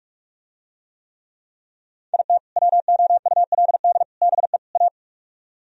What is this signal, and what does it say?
Morse code, heard as a keyed tone, about 45 wpm, 700 Hz. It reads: IT WOULD BE A